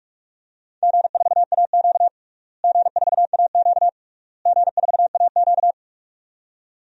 Morse code, heard as a keyed tone, about 46 wpm, 690 Hz. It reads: G4AQ G4AQ G4AQ
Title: DXCC-100 Call Signs - 4 Characters - Repeated 3x - 45wpm